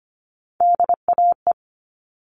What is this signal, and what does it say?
Morse code, heard as a keyed tone, about 25 wpm, 700 Hz. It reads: DAE
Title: All Cut Numbers - 3-Digit 25wpm